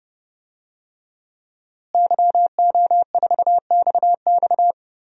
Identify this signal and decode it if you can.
Morse code, heard as a keyed tone, about 30 wpm, 700 Hz. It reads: YO4XX
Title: DXCC-100 Call Signs - 5 Characters - Repeated 3x - 30wpm